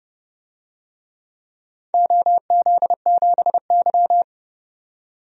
Morse code, heard as a keyed tone, about 31 wpm, 700 Hz. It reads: OZ7Y
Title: DXCC-100 Call Signs - 4 Characters - 30wpm